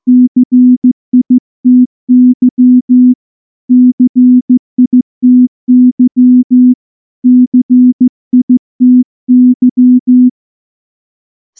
Morse code, heard as a keyed tone, about 14 wpm, 250 Hz. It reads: CITY CITY CITY
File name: morse / morse-training-15WPM